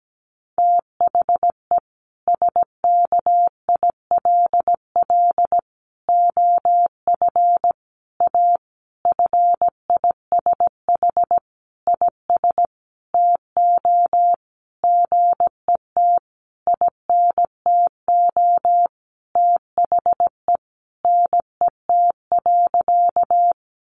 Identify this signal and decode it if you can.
Morse code, heard as a keyed tone, about 17 wpm, 700 Hz. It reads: THE SKILL OF A FISH IS TO GET INTO THE NET.
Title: Sentences from Top 1400 Words 17wpm